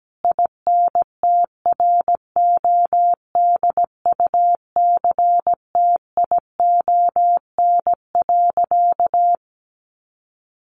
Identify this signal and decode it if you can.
Morse code, heard as a keyed tone, about 17 wpm, 700 Hz. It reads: INTRODUCTION.
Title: Sentences from Top 1000 Words 17wpm